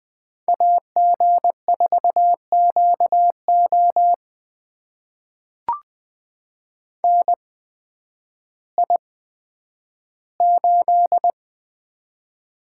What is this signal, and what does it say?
Morse code, heard as a keyed tone, about 20 wpm, spaced as for 11 wpm, 700 Hz. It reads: AG4QO NI8
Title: USA Call Signs - 8x Character Spacing - 20wpm